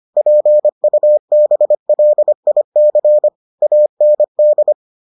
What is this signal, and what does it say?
Morse code, heard as a keyed tone, about 25 wpm, 600 Hz. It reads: PUBLIC AND